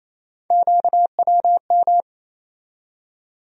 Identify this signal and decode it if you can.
Morse code, heard as a keyed tone, about 28 wpm, 700 Hz. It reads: QWM
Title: USA Call Sign Postfixes - 3 Letters 28wpm